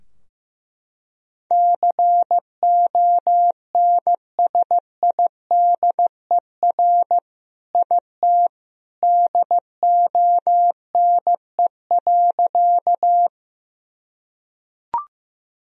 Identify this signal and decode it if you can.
Morse code, heard as a keyed tone, about 16 wpm, 700 Hz. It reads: CONSIDER IT DONE.
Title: Sentences from Top 1000 Words - Review Part 4 - 15wpm